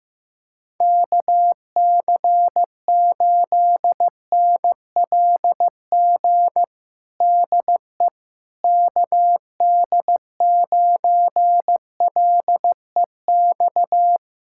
Morse code, heard as a keyed tone, about 16 wpm, 700 Hz. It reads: KC8NLG DE KD9LEX
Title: USA Callsign DE Callsign - 6 Characters - 15wpm